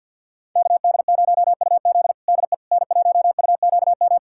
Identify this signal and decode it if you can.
Morse code, heard as a keyed tone, about 51 wpm, 700 Hz. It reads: KD0UZ DE N1UQM
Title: USA Callsign DE Callsign - 5 Characters - 50wpm